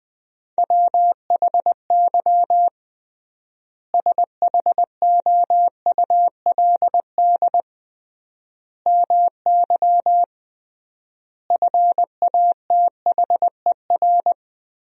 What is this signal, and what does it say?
Morse code, heard as a keyed tone, about 20 wpm, 700 Hz. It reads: WHY SHOULD MY FATHER
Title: 3x Word-Spacing - Sentences from Top 500 Words - 20wpm